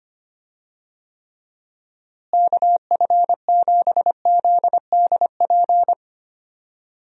Morse code, heard as a keyed tone, about 25 wpm, 700 Hz. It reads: KF7ZDP
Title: USA Call Signs - 6 Characters - Repeated 3x - 25wpm